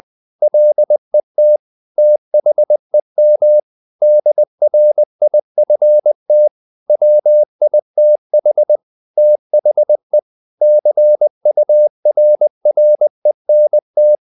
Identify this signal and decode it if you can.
Morse code, heard as a keyed tone, about 20 wpm, 600 Hz. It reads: LET THEM DRIFT WITH THE CURRENT
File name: the old man and the sea / the old man and the sea part 1 20wpm